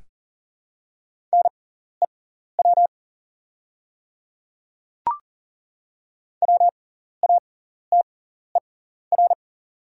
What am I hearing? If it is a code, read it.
Morse code, heard as a keyed tone, about 40 wpm, spaced as for 14 wpm, 700 Hz. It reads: NEW WATER
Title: 6x Character Spacing - Top 200 Words - 40wpm